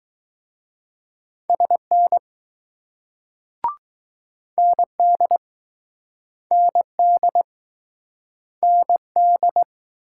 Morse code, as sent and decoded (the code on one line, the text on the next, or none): ... -. / -. -.. / -. -.. / -. -..
SN ND ND ND